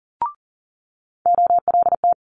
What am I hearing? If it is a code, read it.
Morse code, heard as a keyed tone, about 42 wpm, 700 Hz. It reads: OLT
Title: Multi-letter Phonemes Quick Fire 40wpm